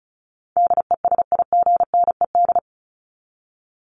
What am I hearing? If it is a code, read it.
Morse code, heard as a keyed tone, about 34 wpm, 700 Hz. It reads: DESIGNED
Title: ED Suffix - Top 100 Words with Base Words 35wpm